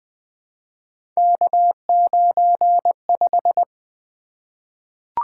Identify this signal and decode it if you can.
Morse code, heard as a keyed tone, about 20 wpm, 700 Hz. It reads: K95
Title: Three Letter-Number Combinations - 20wpm